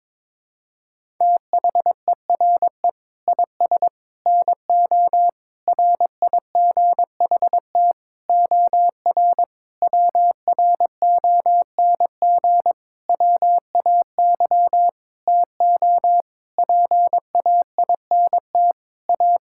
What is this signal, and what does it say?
Morse code, heard as a keyed tone, about 22 wpm, 700 Hz. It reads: THERE IS NO RIGHT OR WRONG WAY TO PAINT A
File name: practice / 22wpm / Sentences with 8-Word Phrases - 22wpm